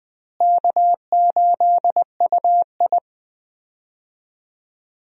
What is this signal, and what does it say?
Morse code, heard as a keyed tone, about 19 wpm, 700 Hz. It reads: K8UI
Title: USA Call Signs - 4 Characters - Repeated 3x - 20wpm